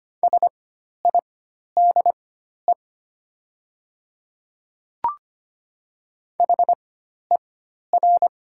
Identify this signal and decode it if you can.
Morse code, heard as a keyed tone, about 25 wpm, spaced as for 12 wpm, 690 Hz. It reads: SIDE HER
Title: 4x Character Spacing - Top 100 Words - 25wpm